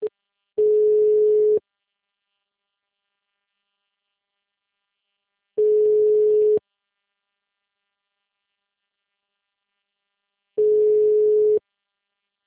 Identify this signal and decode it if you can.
Ringback tone (Europe, 425 Hz).